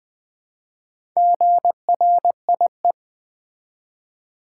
Morse code, heard as a keyed tone, about 20 wpm, 700 Hz. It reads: GRIE
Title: Build A Word - 1 Letter At A Time - 5-Letter Words - 20wpm